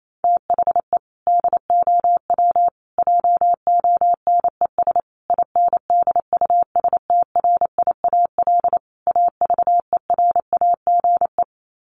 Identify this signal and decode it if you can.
Morse code, heard as a keyed tone, about 28 wpm, 700 Hz. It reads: THE DOW JONES INDUSTRIAL AVERAGE
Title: Sets of 5 Words 28wpm